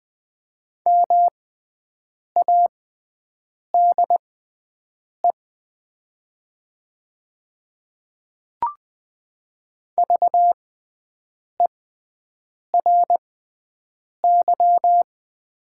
Morse code, heard as a keyed tone, about 20 wpm, spaced as for 7 wpm, 700 Hz. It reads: MADE VERY
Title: 6x Character Spacing - Top 200 Words - 20wpm